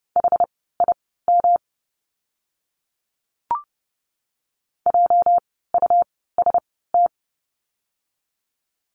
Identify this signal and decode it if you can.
Morse code, heard as a keyed tone, about 30 wpm, spaced as for 17 wpm, 700 Hz. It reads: HIM JUST
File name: course / 30:10wpm / 237 Lesson 38f - Letter q - Words 30wpm